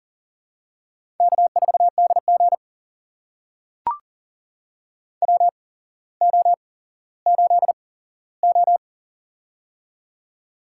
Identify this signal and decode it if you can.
Morse code, heard as a keyed tone, about 38 wpm, spaced as for 17 wpm, 700 Hz. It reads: K4DG WO8O
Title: USA Call Signs - 4 Characters - 8x Character Spacing - 40wpm